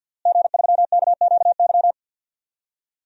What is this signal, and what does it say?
Morse code, heard as a keyed tone, about 52 wpm, 700 Hz. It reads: G3KQY
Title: DXCC-100 Call Signs - 5 Characters - Repeated 3x - 50wpm